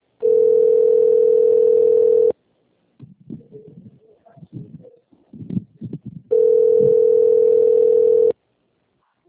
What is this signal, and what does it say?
Ringback tone (North America), 440 + 480 Hz.
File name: rg-1600-unknown- - - .1006